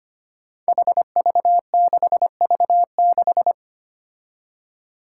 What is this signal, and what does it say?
Morse code, heard as a keyed tone, about 25 wpm, 700 Hz. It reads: HV6V6
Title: Five Challenging Letter-Number Combinations 25wpm